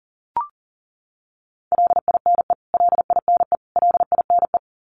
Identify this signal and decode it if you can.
Morse code, heard as a keyed tone, about 40 wpm, 700 Hz. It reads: LINE LINE LINE
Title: Top 200 Words Repeated 40wpm